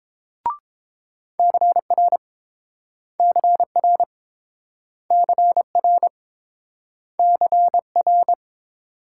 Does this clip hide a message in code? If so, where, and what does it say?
Yes: Morse code from 1.4 to 8.3 s, heard as a keyed tone, about 30 wpm, 700 Hz. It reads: CR CR CR CR